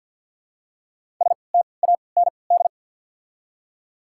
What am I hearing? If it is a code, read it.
Morse code, heard as a keyed tone, about 49 wpm, 700 Hz. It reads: S T A N D
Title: 3x Character Spacing - Top 300 Words - 50wpm